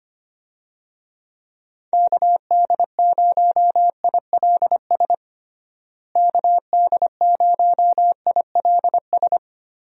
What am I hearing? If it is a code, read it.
Morse code, heard as a keyed tone, about 25 wpm, 700 Hz. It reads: KD0ILS KD0ILS
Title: USA Call Signs - 6 Characters - Repeated 3x - 25wpm